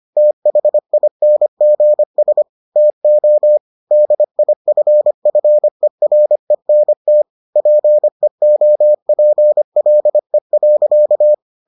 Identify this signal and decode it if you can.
Morse code, heard as a keyed tone, about 25 wpm, 600 Hz. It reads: THINGS TO DIFFERENT PEOPLE.